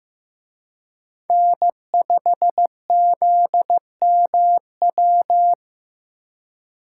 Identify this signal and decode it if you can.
Morse code, heard as a keyed tone, about 18 wpm, 700 Hz. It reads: N5ZMW
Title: Speed Racing - USA 1x3 Call Signs 15wpm